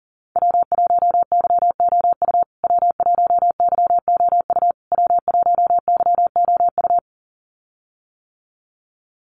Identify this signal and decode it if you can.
Morse code, heard as a keyed tone, about 40 wpm, 700 Hz. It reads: W1YOU W1YOU W1YOU
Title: USA Call Signs Repeated 3x 40wpm